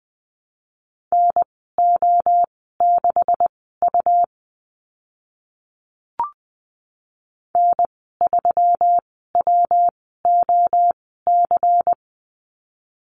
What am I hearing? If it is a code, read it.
Morse code, heard as a keyed tone, about 20 wpm, spaced as for 14 wpm, 700 Hz. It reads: NO6U N3WOC
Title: 140 Lesson 23g - Letter f - Call Signs 20wpm